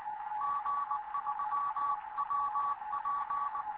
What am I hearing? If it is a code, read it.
Morse code, heard as a keyed tone, about 20 wpm, 1140 Hz, with radio noise. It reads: G3WP
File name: Morze-HA2MN / hangp / 2 11 GYAKORLAT